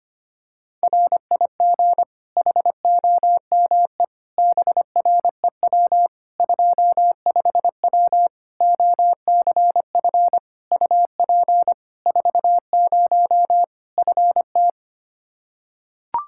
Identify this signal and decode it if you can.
Morse code, heard as a keyed tone, about 25 wpm, 700 Hz. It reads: RIG HOME BREW 25W OCF UP 40 FT
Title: Practice QSO - Exchange 2 - Info - 25wpm